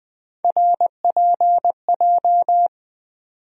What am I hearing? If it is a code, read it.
Morse code, heard as a keyed tone, about 21 wpm, 700 Hz. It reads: RPJ
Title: USA Call Sign Postfixes - 3 Letters 20wpm